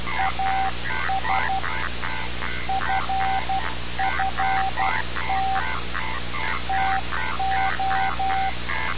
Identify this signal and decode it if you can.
Morse code, heard as a keyed tone, about 12 wpm, 770 Hz. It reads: AS FFT TO
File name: radiocap / REC z